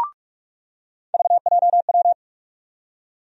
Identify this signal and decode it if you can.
Morse code, heard as a keyed tone, about 46 wpm, 700 Hz. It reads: VJW
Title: USA Call Sign Postfixes - 3 Letters 45wpm